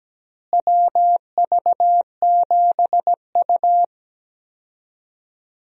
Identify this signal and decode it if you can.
Morse code, heard as a keyed tone, about 17 wpm, 700 Hz. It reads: WV7U
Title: USA Call Signs - 4 Characters - Repeated 3x - 17wpm